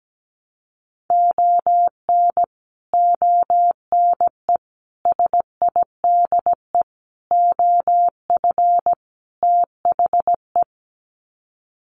Morse code, heard as a keyed tone, about 17 wpm, 700 Hz. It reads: ON ONE SIDE OF THE
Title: Sets of 5 Words 17wpm